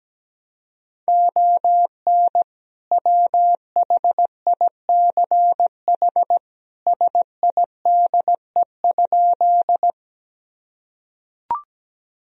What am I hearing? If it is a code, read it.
Morse code, heard as a keyed tone, about 17 wpm, 700 Hz. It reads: ON WHICH SIDE?
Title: Sentences from Top 1000 Words - Review Part 1 - 17wpm